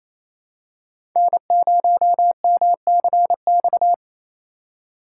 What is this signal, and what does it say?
Morse code, heard as a keyed tone, about 28 wpm, 700 Hz. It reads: N0MCX
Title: USA Call Signs - 5 Characters - 4x Character Spacing - 28wpm